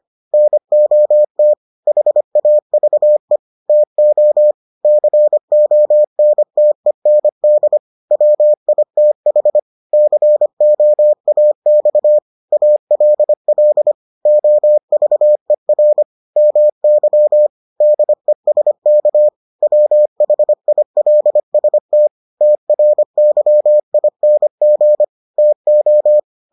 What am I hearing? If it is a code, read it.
Morse code, heard as a keyed tone, about 25 wpm, 600 Hz. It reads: NOT HAVE TO CONTEND WITH COAX ALL OVER MY DESK WHILST TRYING TO